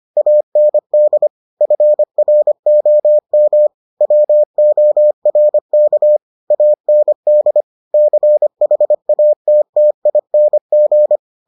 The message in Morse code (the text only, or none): AND FROM WORK AND CHATTING